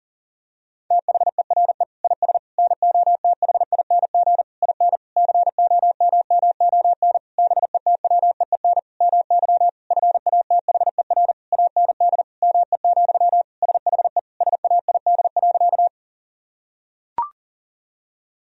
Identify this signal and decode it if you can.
Morse code, heard as a keyed tone, about 40 wpm, 700 Hz. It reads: THERE IS NOTHING IN COMMON BETWEEN MY FATHER AND ME, SHE SAID.
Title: Sentences from Top 1000 Words - Review Part 3 - 40wpm